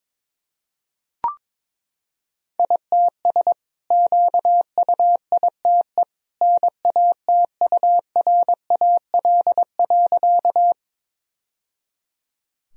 Morse code, heard as a keyed tone, about 22 wpm, 700 Hz. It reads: ITS QUITE NATURAL.
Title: Sentences from Top 1000 Words - Review Part 4 - 22wpm